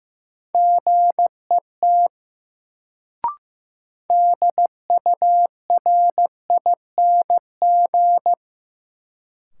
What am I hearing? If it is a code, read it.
Morse code, heard as a keyed tone, about 15 wpm, 700 Hz. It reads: GET DURING
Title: Sets of 2 Words - New Word Drill - 15wpm